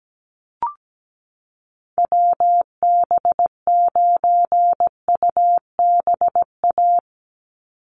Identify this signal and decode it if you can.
Morse code, heard as a keyed tone, about 17 wpm, 700 Hz. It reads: WB9UBA
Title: USA Call Signs 17wpm